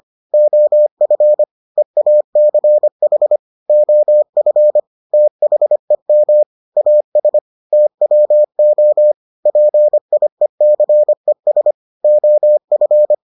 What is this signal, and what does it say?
Morse code, heard as a keyed tone, about 25 wpm, 600 Hz. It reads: OF EACH OF THEM AS TWO PIECES OF